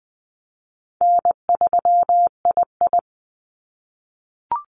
Morse code, heard as a keyed tone, about 20 wpm, 700 Hz. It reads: N3II
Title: NA SOTA Activator Callsigns 20wpm